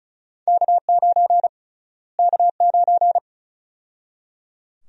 Morse code, heard as a keyed tone, about 36 wpm, 700 Hz. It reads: K9 K9
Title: USA Call Sign Prefixes - Repeated 2x - 35wpm